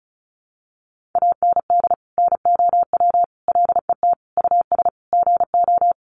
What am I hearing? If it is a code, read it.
Morse code, heard as a keyed tone, about 34 wpm, 700 Hz. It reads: AND NOW LET US GO
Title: Sentences from Top 1500 Words 35wpm